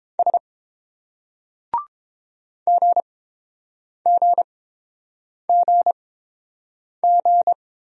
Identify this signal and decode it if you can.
Morse code, heard as a keyed tone, about 30 wpm, spaced as for 7 wpm, 700 Hz. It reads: S GGGG